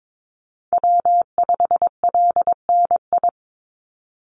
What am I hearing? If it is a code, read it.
Morse code, heard as a keyed tone, about 22 wpm, 700 Hz. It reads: W5LNI
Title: Call Signs - 22wpm